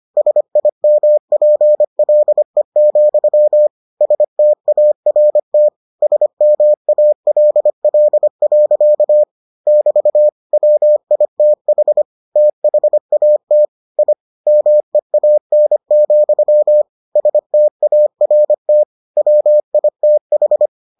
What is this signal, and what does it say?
Morse code, heard as a keyed tone, about 25 wpm, 600 Hz. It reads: SIMPLE, START SMALL. <BT> WITH THAT I MEAN, START WITH